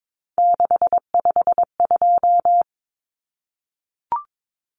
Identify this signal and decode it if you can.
Morse code, heard as a keyed tone, about 22 wpm, 700 Hz. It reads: 652
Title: Common Cut Numbers - 3-Digit 22wpm